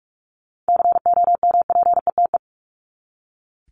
Morse code, heard as a keyed tone, about 44 wpm, 700 Hz. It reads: COMPETE
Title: Top 1751-2000 Words 45wpm